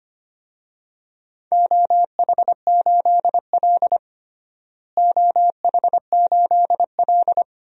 Morse code, heard as a keyed tone, about 25 wpm, 700 Hz. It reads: OH8L OH8L
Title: DXCC-100 Call Signs - 4 Characters - Repeated 2x - 25wpm